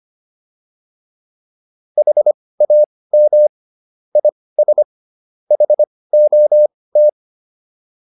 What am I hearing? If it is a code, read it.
Morse code, heard as a keyed tone, about 25 wpm, spaced as for 18 wpm, 600 Hz. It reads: HAM IS HOT